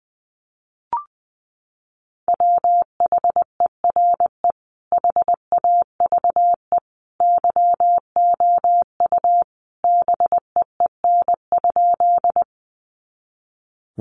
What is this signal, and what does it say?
Morse code, heard as a keyed tone, about 20 wpm, 700 Hz. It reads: WHERE HAVE YOU BEEN?
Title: Sentences from Top 200 Words 20wpm